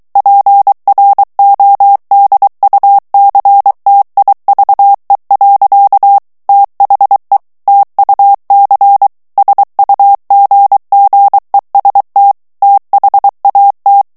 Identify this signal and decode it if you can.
Morse code, heard as a keyed tone, about 23 wpm, 800 Hz. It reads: PRODUCTIVE. THE TUC SUGGEST THAT
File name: morse mp3 / ANGLAIS texte clair 1 vitesse 22 mots minute